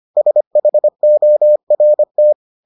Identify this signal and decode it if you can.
Morse code, heard as a keyed tone, about 25 wpm, 600 Hz. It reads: SHORT